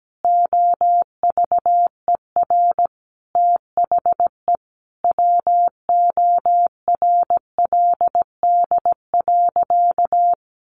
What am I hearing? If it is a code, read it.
Morse code, heard as a keyed tone, about 17 wpm, 700 Hz. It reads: OVER THE WORLD.
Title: Additional Easy Phrases 17wpm